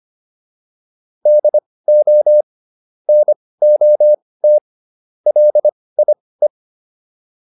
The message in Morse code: DO NOT LIE

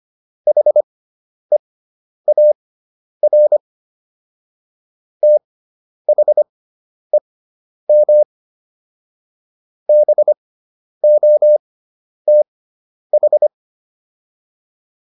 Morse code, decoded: HEAR THEM BOTH

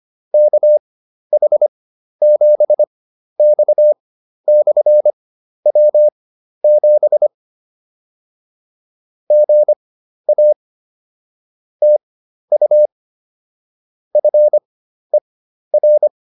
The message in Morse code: KH7X/W7 GA TU FER

